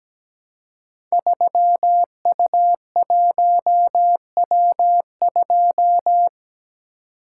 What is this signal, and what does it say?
Morse code, heard as a keyed tone, about 17 wpm, 700 Hz. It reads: 3U1W2
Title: Code Groups - 5 Letter:Numbers - 17wpm